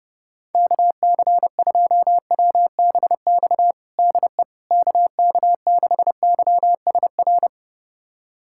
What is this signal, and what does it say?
Morse code, heard as a keyed tone, about 32 wpm, 700 Hz. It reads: KC2WBX DE KK6YSR
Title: USA Callsign DE Callsign - 6 Characters - 30wpm